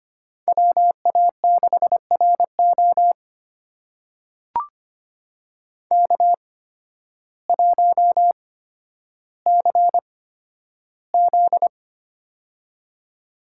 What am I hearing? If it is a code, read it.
Morse code, heard as a keyed tone, about 25 wpm, spaced as for 12 wpm, 700 Hz. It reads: WA6RO K1CZ
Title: USA Call Signs - 8x Character Spacing - 25wpm